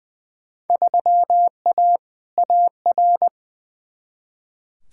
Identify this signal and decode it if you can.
Morse code, heard as a keyed tone, about 20 wpm, 700 Hz. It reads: 3A AR
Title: Field Day Exchanges - 20wpm